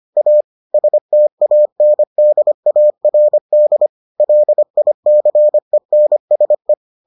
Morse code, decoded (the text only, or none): A STANDARD LICENSE